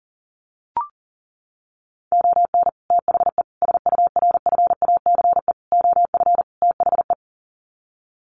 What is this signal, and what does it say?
Morse code, heard as a keyed tone, about 40 wpm, 700 Hz. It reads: ON THE SURFACE OF THE
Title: Sets of 5 Words Encore 40wpm